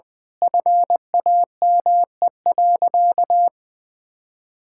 Morse code, heard as a keyed tone, about 21 wpm, 700 Hz. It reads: FAME.
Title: Sentences with 6-Word Phrases - Spoken Beginning - 20wpm